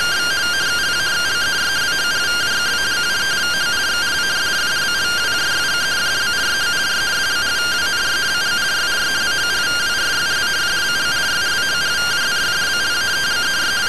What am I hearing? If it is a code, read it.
Two-tone teleprinter signal, about 170 Hz shift.
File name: findings / CT AXIAL -142.gen-03 rtty rx